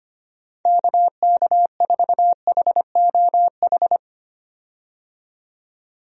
Morse code, heard as a keyed tone, about 25 wpm, 700 Hz. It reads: KK4HOH
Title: USA Call Signs - 6 Characters - 25wpm